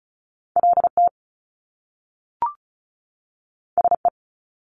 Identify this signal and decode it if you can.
Morse code, heard as a keyed tone, about 36 wpm, 700 Hz. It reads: LT SE
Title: Top Two Letter Combinations in Words 35wpm